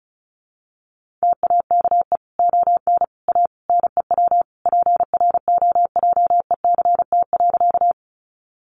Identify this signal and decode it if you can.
Morse code, heard as a keyed tone, about 34 wpm, 700 Hz. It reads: TAKE ON A NEW PROJECT.
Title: Sentences from Top 1400 Words 35wpm